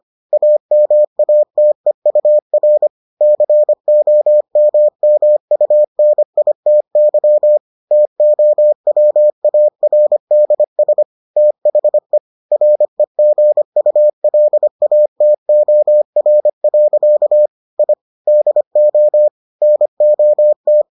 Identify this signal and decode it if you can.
Morse code, heard as a keyed tone, about 25 wpm, 600 Hz. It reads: AMATEUR COMMUNITY TOWARDS THE REGULATOR. I DO NOT